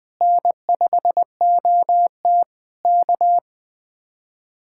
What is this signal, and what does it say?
Morse code, heard as a keyed tone, about 20 wpm, 700 Hz. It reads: N5OT K
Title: Speed-Racing Common QSO Phrases Encore - 17wpm